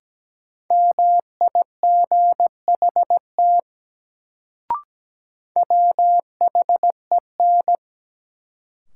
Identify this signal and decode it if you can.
Morse code, heard as a keyed tone, about 17 wpm, 700 Hz. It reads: MIGHT WHEN
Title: Sets of 2 Words - New Word Drill - 17wpm